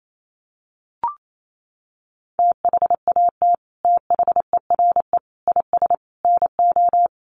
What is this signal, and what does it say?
Morse code, heard as a keyed tone, about 28 wpm, 700 Hz. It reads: THAT THERE IS NO
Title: Sets of 4 Words 28wpm